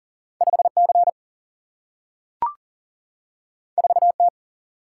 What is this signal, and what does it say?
Morse code, heard as a keyed tone, about 39 wpm, 700 Hz. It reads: 5C 4T
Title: Single-Letter Single-Number 40wpm